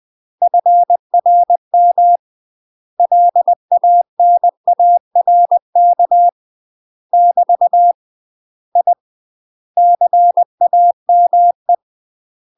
Morse code, heard as a keyed tone, about 20 wpm, 700 Hz. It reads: FRM LANARK <BT> I CAME
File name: UK MASTER QSOs TRAINING / 20WPM / UK MASTER QSOs 0057